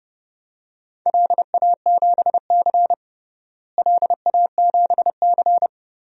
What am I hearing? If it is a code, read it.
Morse code, heard as a keyed tone, about 30 wpm, 700 Hz. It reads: LA7C LA7C
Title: DXCC-100 Call Signs - 4 Characters - Repeated 2x - 30wpm